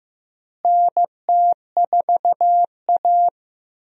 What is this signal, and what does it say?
Morse code, heard as a keyed tone, about 16 wpm, 700 Hz. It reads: NT4A
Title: USA Call Signs - 4 Characters - Repeated 2x - 15wpm